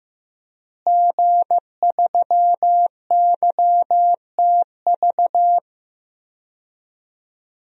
Morse code, heard as a keyed tone, about 15 wpm, 700 Hz. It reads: G3YTV
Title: DXCC-100 Call Signs - 5 Characters - Repeated 3x - 15wpm